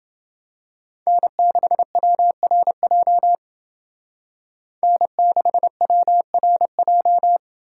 Morse code, heard as a keyed tone, about 29 wpm, 700 Hz. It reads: N6WRJ N6WRJ